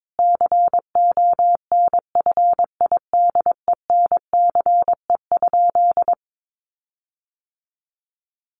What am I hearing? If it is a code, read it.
Morse code, heard as a keyed tone, about 22 wpm, 700 Hz. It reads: CONFIDENCE?
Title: Sentences from Top 1500 Words 22wpm